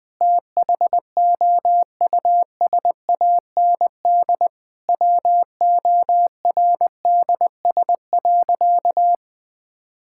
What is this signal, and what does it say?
Morse code, heard as a keyed tone, about 20 wpm, 700 Hz. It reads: THOUSAND WORDS.
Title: Sentences with 7-Word Phrases - 20wpm